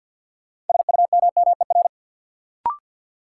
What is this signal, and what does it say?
Morse code, heard as a keyed tone, about 51 wpm, 700 Hz. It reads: SUMMER
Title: Sentences from Top 1100 Words - New Word Drill Alt - 50wpm